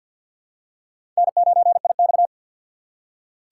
Morse code, heard as a keyed tone, about 52 wpm, 700 Hz. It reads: N9IX
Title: USA Call Signs - 3x Character Spacing - 50wpm